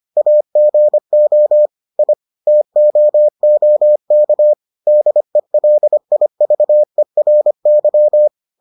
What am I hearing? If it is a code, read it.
Morse code, heard as a keyed tone, about 25 wpm, 600 Hz. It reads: AGO I TOOK DELIVERY